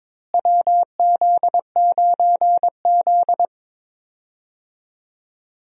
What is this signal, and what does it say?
Morse code, heard as a keyed tone, about 22 wpm, 700 Hz. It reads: WZ9Z
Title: USA Call Signs - 4 Characters - Repeated 3x - 22wpm